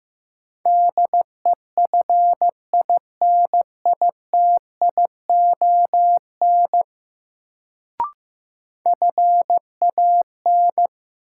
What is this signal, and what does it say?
Morse code, heard as a keyed tone, about 16 wpm, 700 Hz. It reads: DEFINITION FAN
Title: Sentences from Top 1000 Words - Review Part 4 - New Word Drill - 15wpm